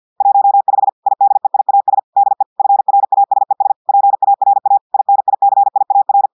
Morse code, heard as a keyed tone, about 50 wpm, 800 Hz. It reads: 15 IDEIAS DE FRASES PARA INICIAR
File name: portugais texte en clair 2 vitesse 50 mots minutes